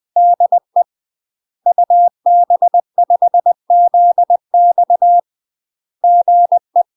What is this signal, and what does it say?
Morse code, heard as a keyed tone, about 20 wpm, 700 Hz. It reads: DE UB5ZX GE
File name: UK MASTER QSOs TRAINING / 20WPM / UK MASTER QSOs 0041